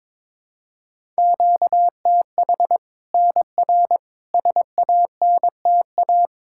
Morse code, heard as a keyed tone, about 22 wpm, 700 Hz. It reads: QTH NR SANTA
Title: Practice QSO - Exchange 1 - Beginning - 22wpm